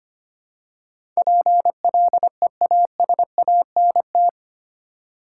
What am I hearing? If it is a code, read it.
Morse code, heard as a keyed tone, about 25 wpm, 700 Hz. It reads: PLEASANT
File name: 8 Letter Words - First 3 Letters Given - 25wpm